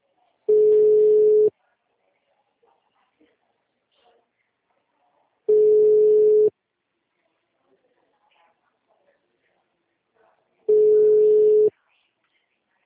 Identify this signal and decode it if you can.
Ringback tone (Europe, 425 Hz).